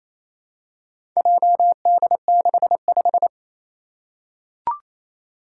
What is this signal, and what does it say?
Morse code, heard as a keyed tone, about 27 wpm, 700 Hz. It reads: JD65
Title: Four Challenging Letter-Number Combinations 28wpm